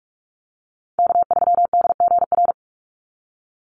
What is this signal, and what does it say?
Morse code, heard as a keyed tone, about 44 wpm, 700 Hz. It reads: K3DGR
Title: Call Signs - 45wpm